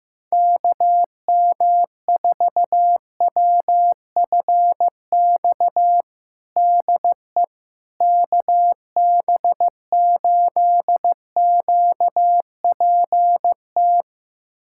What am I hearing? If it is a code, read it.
Morse code, heard as a keyed tone, about 15 wpm, 700 Hz. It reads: KM4WFX DE KB8QPT
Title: USA Callsign DE Callsign - 6 Characters - 15wpm